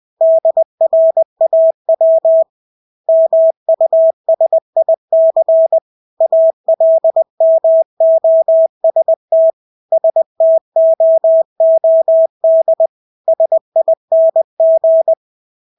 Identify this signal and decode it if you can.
Morse code, heard as a keyed tone, about 20 wpm, 650 Hz. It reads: DRAW MUSIC ALMOST STOOD SING
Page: Engelske ord 20 wpm